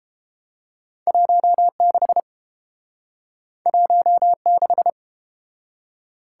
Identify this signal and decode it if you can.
Morse code, heard as a keyed tone, about 31 wpm, 700 Hz. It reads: 16 16